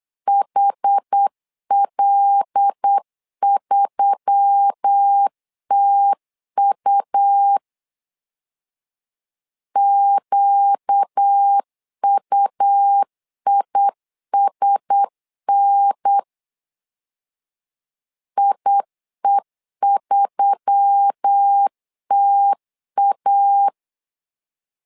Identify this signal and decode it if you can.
Morse code, heard as a keyed tone, about 8 wpm, 800 Hz. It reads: HL3TU QUISN IE3TA
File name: LEZIONI / 5 LBVC / 5 L 8wpm 2831